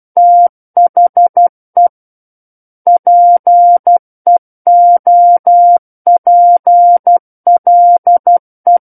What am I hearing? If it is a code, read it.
Morse code, heard as a keyed tone, about 12 wpm, 700 Hz. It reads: THE PEOPLE
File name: Book One / 03-Chapter 3 - 12 WPM